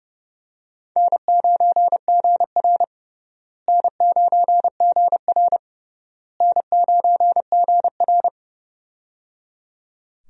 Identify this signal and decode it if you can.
Morse code, heard as a keyed tone, about 30 wpm, 700 Hz. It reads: N9GR N9GR N9GR
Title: USA Call Signs - 4 Characters - Repeated 3x - 30wpm